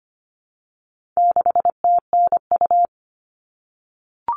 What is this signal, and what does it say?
Morse code, heard as a keyed tone, about 25 wpm, 700 Hz. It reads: 6TNU
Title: All Cut Numbers - 4-Digit 25wpm